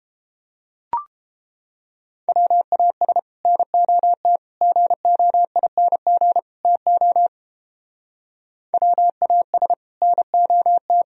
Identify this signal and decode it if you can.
Morse code, heard as a keyed tone, about 29 wpm, 700 Hz. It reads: WAS NOT GOING TO WAS NOT